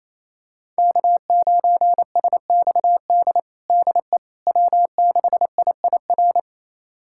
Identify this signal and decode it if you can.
Morse code, heard as a keyed tone, about 28 wpm, 700 Hz. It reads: K9SXD DE W6IIR
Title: USA Callsign DE Callsign - 5 Characters - 28wpm